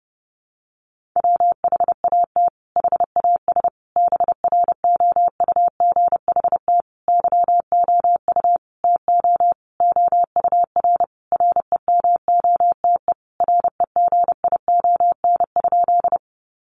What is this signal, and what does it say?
Morse code, heard as a keyed tone, about 30 wpm, 700 Hz. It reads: WHAT HAS BROUGHT YOU TO OUR REMOTE REGION?
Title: Sentences from Top 1500 Words 30wpm